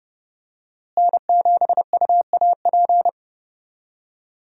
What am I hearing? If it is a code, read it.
Morse code, heard as a keyed tone, about 32 wpm, 700 Hz. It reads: N7UAP
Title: USA Call Signs - 5 Characters - Repeated 3x - 30wpm